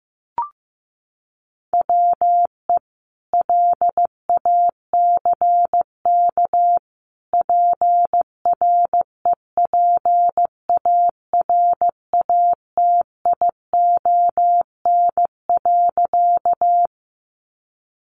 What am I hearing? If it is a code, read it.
Morse code, heard as a keyed tone, about 15 wpm, 700 Hz. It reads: WE LACK PREPARATION.
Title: Sentences from Top 1400 Words 15wpm